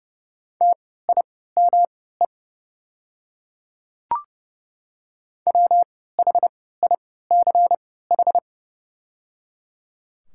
Morse code, heard as a keyed tone, about 30 wpm, spaced as for 17 wpm, 700 Hz. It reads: TIME WHICH
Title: 3x Character Spacing - Top 200 Words - 30wpm